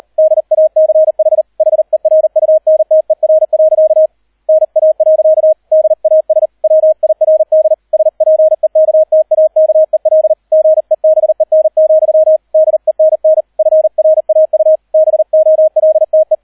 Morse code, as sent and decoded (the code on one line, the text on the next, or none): -.. .- -.-. .... / .... .-. ..- -. - . .-. .-.-.- / -. .- .-.-.- / -.. .- ... / .-- .. .-. -.. / ... .--. -.- - .- -.- . .-.. / --. -... -. --..-- / -.. -. -. / ..-. .-. .- ..- / -... --- .-.. - .
DACH HERUNTER. NA. DAS WIRD SPEKTAKEL GEBEN, DENN FRAU BOLTE